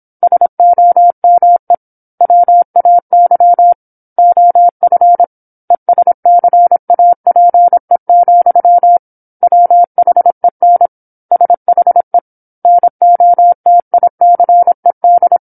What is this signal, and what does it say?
Morse code, heard as a keyed tone, about 27 wpm, 700 Hz. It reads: SOME WAY OF ESCAPE, WHEN SHE NOTICED
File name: 08-Chapter-08 26 WPM